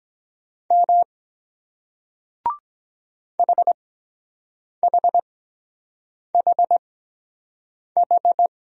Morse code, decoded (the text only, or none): M HHHH